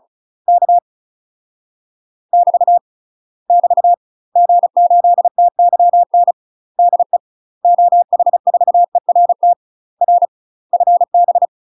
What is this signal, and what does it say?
Morse code, heard as a keyed tone, about 34 wpm, 700 Hz. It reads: K <BT> <BT> G8TYN DE OH4ERT R FB
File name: UK MASTER QSOs TRAINING/35WPM